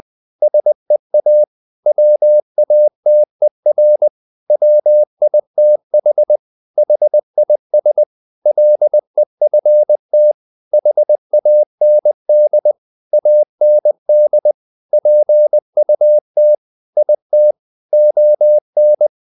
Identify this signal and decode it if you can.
Morse code, heard as a keyed tone, about 20 wpm, 600 Hz. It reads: SEA WATER WITH HIS LEFT HAND AND PUT IT ON
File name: the old man and the sea part 2 20wpm